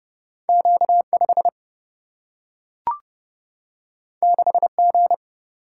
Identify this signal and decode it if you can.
Morse code, heard as a keyed tone, about 30 wpm, 700 Hz. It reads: Q 5 6 G
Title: Single-Letter Single-Number 30wpm